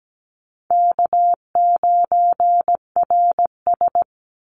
Morse code, heard as a keyed tone, about 17 wpm, 700 Hz. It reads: K9RS
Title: Call Signs - 17wpm